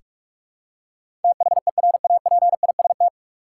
Morse code, heard as a keyed tone, about 46 wpm, 700 Hz. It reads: THERAPIST
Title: Practice QSO - Exchange 3 - More Info - 45wpm